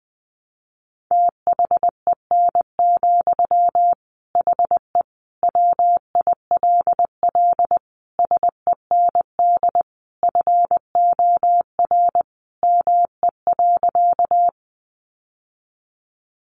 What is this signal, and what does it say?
Morse code, heard as a keyed tone, about 20 wpm, 700 Hz. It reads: THEN, HE WILL SEND FOR ME.
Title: Sentences from Top 1500 Words 20wpm